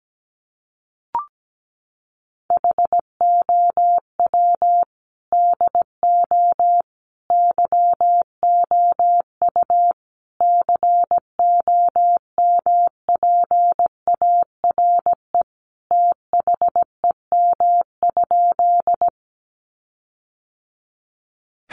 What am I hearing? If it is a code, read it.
Morse code, heard as a keyed tone, about 17 wpm, 700 Hz. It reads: HOW DO YOU COMPARE THEM?
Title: Sentences from Top 1300 Words 17wpm